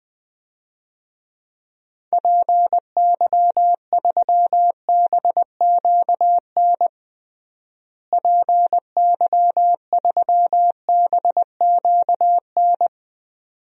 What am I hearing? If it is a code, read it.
Morse code, heard as a keyed tone, about 20 wpm, 700 Hz. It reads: PY3BQN PY3BQN
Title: DXCC-100 Call Signs - 6 Characters - Repeated 2x - 20wpm